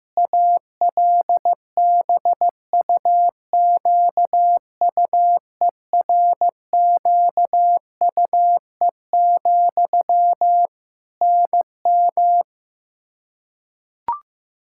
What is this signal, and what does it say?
Morse code, heard as a keyed tone, about 16 wpm, 700 Hz. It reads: ALBUQUERQUE, NM
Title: Practice QSO - Exchange 1 - Beginning - 15wpm